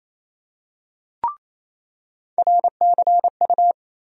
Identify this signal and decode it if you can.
Morse code, heard as a keyed tone, about 27 wpm, 700 Hz. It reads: RCU
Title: USA Call Sign Postfixes - 3 Letters 28wpm